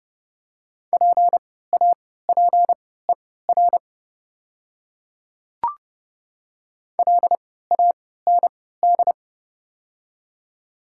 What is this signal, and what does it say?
Morse code, heard as a keyed tone, about 30 wpm, spaced as for 17 wpm, 700 Hz. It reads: PAPER LAND
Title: 3x Character Spacing - Top 300 Words - 30wpm